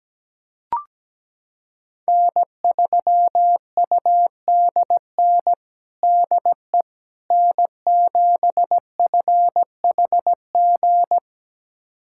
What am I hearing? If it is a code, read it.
Morse code, heard as a keyed tone, about 17 wpm, 700 Hz. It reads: N3UDN DE N7FHG
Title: USA Callsign DE Callsign - 5 Characters - 17wpm